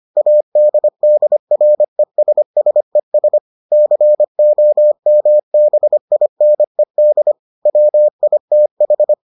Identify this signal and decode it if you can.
Morse code, heard as a keyed tone, about 25 wpm, 600 Hz. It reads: ADDRESSES COMBINED WITH